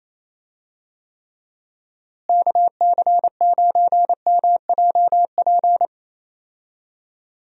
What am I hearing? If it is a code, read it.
Morse code, heard as a keyed tone, about 28 wpm, 700 Hz. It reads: KC9MJP
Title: USA Call Signs - 6 Characters - Repeated 2x - 28wpm